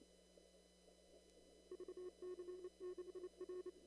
Morse code, heard as a keyed tone, about 30 wpm, 360 Hz. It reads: VC6R
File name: CW / kt5j wpx cw-op-k5pi